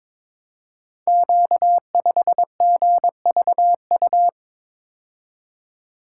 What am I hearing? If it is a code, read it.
Morse code, heard as a keyed tone, about 22 wpm, 700 Hz. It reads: Q5GVU
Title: Code Groups - 5 Letter:Numbers - 22wpm